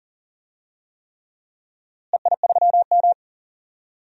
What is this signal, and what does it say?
Morse code, heard as a keyed tone, about 40 wpm, 700 Hz. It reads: EI3M
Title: DXCC-100 Call Signs - 4 Characters - Repeated 3x - 40wpm